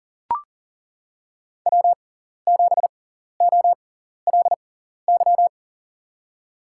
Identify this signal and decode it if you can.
Morse code, heard as a keyed tone, about 40 wpm, spaced as for 14 wpm, 700 Hz. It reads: W7OLY
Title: USA Call Signs - 5 Characters - 6x Character Spacing - 40wpm